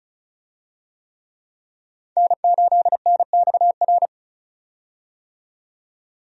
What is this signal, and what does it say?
Morse code, heard as a keyed tone, about 34 wpm, 700 Hz. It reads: N8NXR
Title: USA Call Signs - 5 Characters - 35wpm